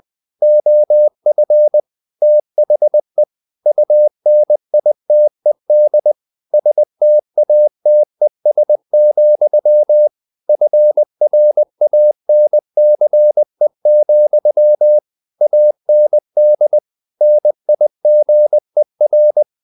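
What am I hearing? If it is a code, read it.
Morse code, heard as a keyed tone, about 20 wpm, 600 Hz. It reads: OF THE UNITED STATES, FRANCE, AND NIGER
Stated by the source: Morse code at 19 wpm from CW med Gnister, reading Ken Follett's novel Never in English.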